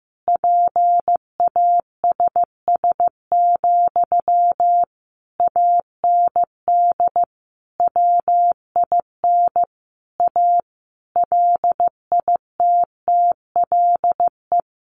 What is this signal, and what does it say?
Morse code, heard as a keyed tone, about 14 wpm, 700 Hz. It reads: PASS, AND WIN A LITTLE
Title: Sentences from Top 1500 Words 15wpm